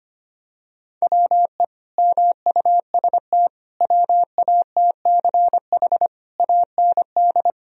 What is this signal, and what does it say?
Morse code, heard as a keyed tone, about 25 wpm, 700 Hz. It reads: WE MUST WATCH AND
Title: Sentences from Top 1000 Words - Review Part 1 - 25wpm